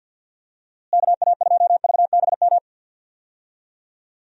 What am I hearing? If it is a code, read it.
Morse code, heard as a keyed tone, about 52 wpm, 700 Hz. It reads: KA2VBM
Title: USA Call Signs - 6 Characters - Repeated 2x - 50wpm